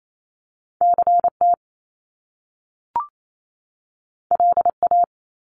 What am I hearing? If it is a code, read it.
Morse code, heard as a keyed tone, about 28 wpm, 700 Hz. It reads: CT LA
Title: US States 28wpm